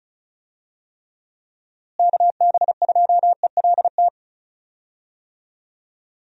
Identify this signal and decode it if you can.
Morse code, heard as a keyed tone, about 34 wpm, 700 Hz. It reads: KB2ELT
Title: USA Call Signs - 6 Characters - Repeated 2x - 35wpm